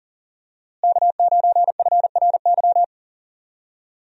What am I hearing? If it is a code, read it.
Morse code, heard as a keyed tone, about 40 wpm, 700 Hz. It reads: K9FRY
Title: USA Call Signs - 5 Characters - Repeated 3x - 40wpm